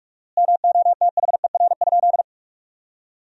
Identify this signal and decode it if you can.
Morse code, heard as a keyed tone, about 47 wpm, 700 Hz. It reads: MOTHER?
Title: 4x Word-Spacing - Sentences from Top 200 Words - 45wpm